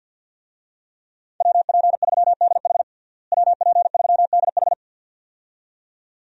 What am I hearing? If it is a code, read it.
Morse code, heard as a keyed tone, about 51 wpm, 700 Hz. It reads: WP3DH WP3DH
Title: USA Call Signs - 5 Characters - Repeated 2x - 50wpm